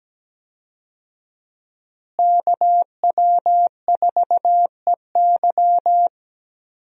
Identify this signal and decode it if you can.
Morse code, heard as a keyed tone, about 17 wpm, 700 Hz. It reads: KW4EY
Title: USA Call Signs - 5 Characters - Repeated 3x - 17wpm